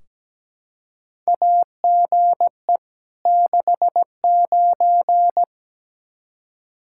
Morse code, heard as a keyed tone, about 17 wpm, 700 Hz. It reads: AGE 69
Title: Practice QSO - Exchange 3 - More Info - 17wpm